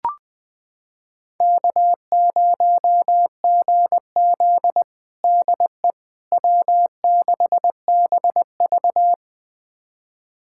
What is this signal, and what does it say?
Morse code, heard as a keyed tone, about 21 wpm, 700 Hz. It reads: K0GZ DE W6BV
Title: USA Callsign DE Callsign - 4 Characters - 20wpm